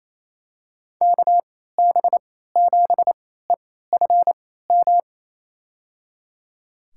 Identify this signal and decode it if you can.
Morse code, heard as a keyed tone, about 28 wpm, spaced as for 16 wpm, 700 Hz. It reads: KB7EFM
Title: USA Call Signs - 6 Characters - 3x Character Spacing - 28wpm